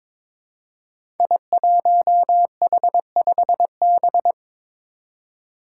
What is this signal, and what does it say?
Morse code, heard as a keyed tone, about 22 wpm, 700 Hz. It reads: I1H5B
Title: Five Challenging Letter-Number Combinations 22wpm